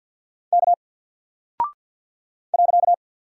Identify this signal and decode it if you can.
Morse code, heard as a keyed tone, about 48 wpm, 700 Hz. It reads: K .
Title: Single Character (Letter:Number:Punctuation:Prosign) Alt 50wpm